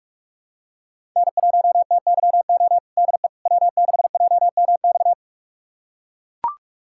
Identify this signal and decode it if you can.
Morse code, heard as a keyed tone, about 42 wpm, 700 Hz. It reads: N1TYO DE W6JMX
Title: USA Callsign DE Callsign - 5 Characters - 45wpm